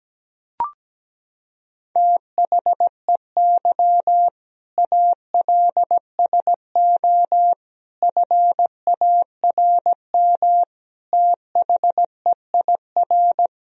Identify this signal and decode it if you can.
Morse code, heard as a keyed tone, about 17 wpm, 700 Hz. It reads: THEY ALSO FARM THEIR
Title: Sentences from Top 1000 Words - Review Part 2 - 17wpm